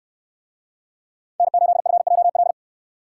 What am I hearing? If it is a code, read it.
Morse code, heard as a keyed tone, about 68 wpm, 700 Hz. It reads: N8FJL